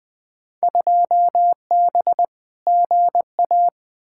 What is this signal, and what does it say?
Morse code, heard as a keyed tone, about 20 wpm, 700 Hz. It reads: 2B GA
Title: Field Day Exchanges Encore - 20wpm